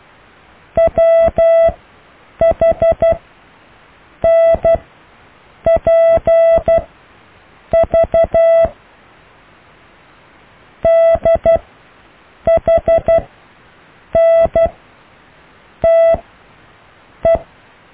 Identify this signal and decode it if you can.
Morse code, heard as a keyed tone, about 12 wpm, spaced as for 6 wpm, 660 Hz. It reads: WHNPV DHNTE
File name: cw / EKTPV+BFHNW